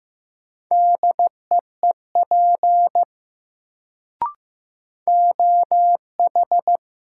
Morse code, heard as a keyed tone, about 15 wpm, 700 Hz. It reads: DEEP OH
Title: Sentences from Top 500 Words - New Word Drill - 15wpm